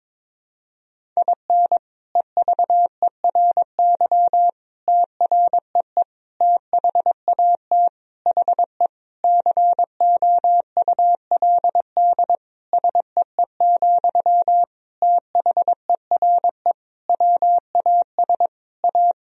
Morse code, heard as a keyed tone, about 22 wpm, 700 Hz. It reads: IN EVERY TREE THAT HE COULD SEE, THERE WAS A
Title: Sentences from Top 1000 Words - Review Part 1 - 22wpm